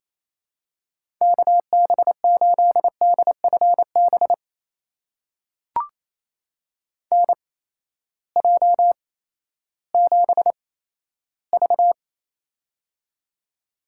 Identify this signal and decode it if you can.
Morse code, heard as a keyed tone, about 28 wpm, spaced as for 14 wpm, 700 Hz. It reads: KB8DFB NJ7V
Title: USA Call Signs - 8x Character Spacing - 28wpm